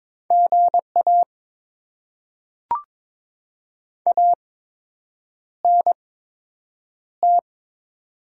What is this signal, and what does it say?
Morse code, heard as a keyed tone, about 22 wpm, spaced as for 8 wpm, 700 Hz. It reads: GA ANT
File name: Top 20 QSO Elements - 8x Character Spacing - 22wpm